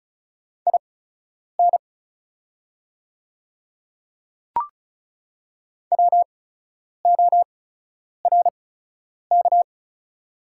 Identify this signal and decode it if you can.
Morse code, heard as a keyed tone, about 36 wpm, spaced as for 10 wpm, 700 Hz. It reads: IN WORK
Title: 8x Character Spacing - Top 300 Words - 35wpm